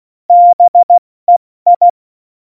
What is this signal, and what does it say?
Morse code, heard as a keyed tone, about 16 wpm, spaced as for 14 wpm, 700 Hz. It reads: BEI
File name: Morsekurs / Witze / Witz 06 March 2018, 70 F80